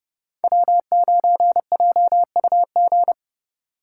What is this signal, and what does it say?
Morse code, heard as a keyed tone, about 31 wpm, 700 Hz. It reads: W9JUG
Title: K1USN SST QSO Practice - 30wpm